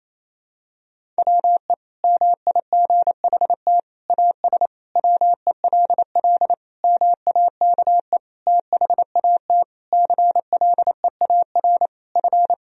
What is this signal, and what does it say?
Morse code, heard as a keyed tone, about 28 wpm, 700 Hz. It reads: WE MIGHT AS WELL MAKE THAT CLEAR F
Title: Sentences from Top 1000 Words - Review Part 1 - 28wpm